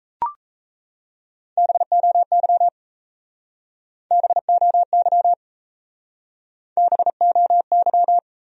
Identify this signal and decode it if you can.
Morse code, heard as a keyed tone, about 37 wpm, 700 Hz. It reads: BOY BOY BOY